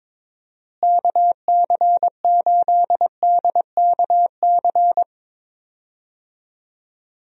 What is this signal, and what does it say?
Morse code, heard as a keyed tone, about 22 wpm, 700 Hz. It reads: KC8DKC
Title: USA Call Signs - 6 Characters - Repeated 3x - 22wpm